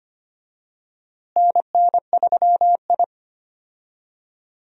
Morse code, heard as a keyed tone, about 25 wpm, 700 Hz. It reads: NN3I
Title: USA Call Signs - 4 Characters - Repeated 3x - 25wpm